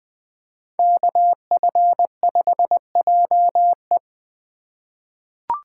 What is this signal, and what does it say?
Morse code, heard as a keyed tone, about 20 wpm, 700 Hz. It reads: KF5JE
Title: USA Call Signs - 5 Characters - 3x Character Spacing - 20wpm